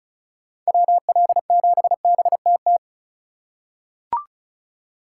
Morse code, heard as a keyed tone, about 36 wpm, 700 Hz. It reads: WL7BTT
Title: USA Call Signs - 4x Character Spacing - 35wpm